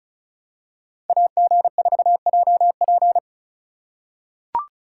Morse code, heard as a keyed tone, about 36 wpm, 700 Hz. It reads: AG4JP
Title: USA Call Signs - 5 Characters - 4x Character Spacing - 35wpm